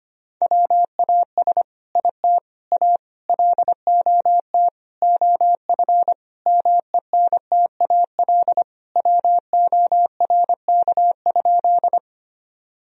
Morse code, heard as a keyed tone, about 25 wpm, 700 Hz. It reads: WAS IT A LOT OF MENTAL WORK?
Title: Sentences from Top 1000 Words - Review Part 4 - 25wpm